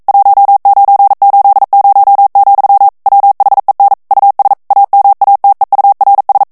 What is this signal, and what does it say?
Morse code, heard as a keyed tone, about 42 wpm, 800 Hz. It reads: 1980, WHEN US AMATEURS